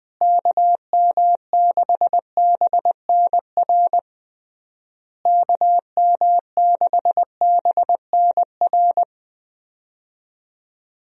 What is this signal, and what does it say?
Morse code, heard as a keyed tone, about 21 wpm, 700 Hz. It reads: KM6BNR KM6BNR
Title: USA Call Signs - 6 Characters - Repeated 3x - 20wpm